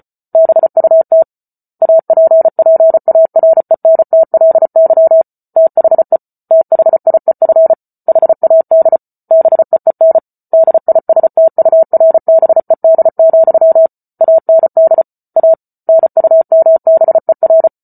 Morse code, heard as a keyed tone, about 34 wpm, 650 Hz. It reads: BUT APPARENTLY THE THIEF HAD BEEN DISTURBED, AND A NUMBER